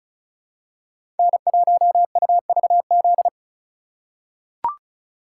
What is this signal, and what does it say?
Morse code, heard as a keyed tone, about 34 wpm, 700 Hz. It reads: N1UVZ
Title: USA Call Signs - 2x Character Spacing - 35wpm